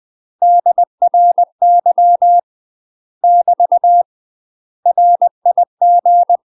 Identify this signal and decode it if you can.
Morse code, heard as a keyed tone, about 20 wpm, 700 Hz. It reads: DRY <BT> RIG
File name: UK MASTER QSOs TRAINING / 20WPM / UK MASTER QSOs 0022